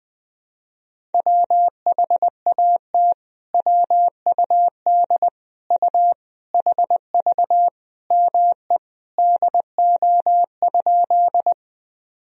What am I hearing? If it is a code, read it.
Morse code, heard as a keyed tone, about 20 wpm, 700 Hz. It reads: WHAT WUD U HV ME DO?
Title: Sentences from Top 200 Words and Abbreviations 20wpm